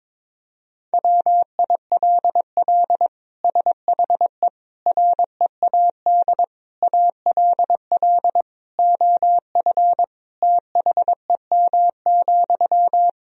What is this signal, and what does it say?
Morse code, heard as a keyed tone, about 22 wpm, 700 Hz. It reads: WILL SHE READ ALL OF THEM,
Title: Sentences from Top 1000 Words - Review Part 4 - 22wpm